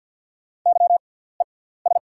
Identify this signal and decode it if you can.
Morse code, heard as a keyed tone, about 48 wpm, spaced as for 17 wpm, 690 Hz. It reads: YES
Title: 6x Character Spacing - Top 500 Words - 50wpm